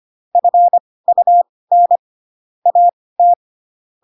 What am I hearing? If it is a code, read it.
Morse code, heard as a keyed tone, about 25 wpm, spaced as for 18 wpm, 700 Hz. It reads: FUN AT
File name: Pinocchio ch3 12 25 ik1hge